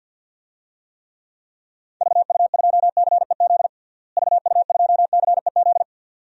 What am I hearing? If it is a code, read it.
Morse code, heard as a keyed tone, about 52 wpm, 700 Hz. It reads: VU2CEZ VU2CEZ
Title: DXCC-100 Call Signs - 6 Characters - Repeated 2x - 50wpm